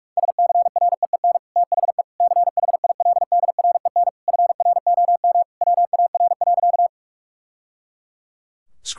Morse code, heard as a keyed tone, about 46 wpm, 700 Hz. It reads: SCREEN THE CHILDREN FROM WAR.
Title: Sentences from Top 1000 Words - Review Part 6 - 45wpm